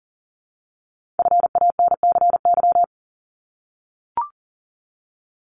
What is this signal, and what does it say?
Morse code, heard as a keyed tone, about 42 wpm, 700 Hz. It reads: FANCY
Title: Multi-letter Phonemes 40wpm